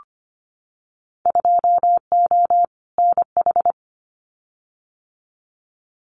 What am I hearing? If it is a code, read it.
Morse code, heard as a keyed tone, about 25 wpm, 700 Hz. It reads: 2O NH
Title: Winter Field Day Exchanges 25wpm